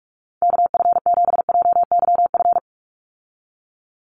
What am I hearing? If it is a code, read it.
Morse code, heard as a keyed tone, about 44 wpm, 700 Hz. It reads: KF7JYF
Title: USA Call Signs 45wpm